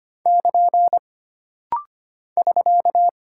Morse code, heard as a keyed tone, about 25 wpm, 700 Hz. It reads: <KN> <SK>
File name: Single Punctuation-Prosign - Rapid-Fire - 25wpm